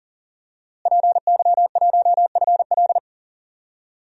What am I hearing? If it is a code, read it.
Morse code, heard as a keyed tone, about 40 wpm, 700 Hz. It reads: PY1FL
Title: DXCC-100 Call Signs - 5 Characters - Repeated 2x - 40wpm